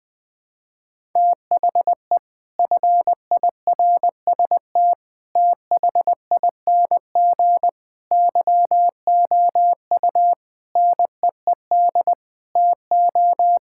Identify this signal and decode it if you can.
Morse code, heard as a keyed tone, about 20 wpm, 700 Hz. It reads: THE FIRST THING YOU NEED TO
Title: Sets of 8 Words - 20wpm